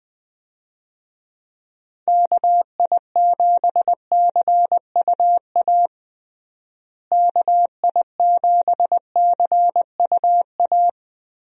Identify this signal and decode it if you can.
Morse code, heard as a keyed tone, about 21 wpm, 700 Hz. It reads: KI7CUA KI7CUA
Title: USA Call Signs - 6 Characters - Repeated 2x - 20wpm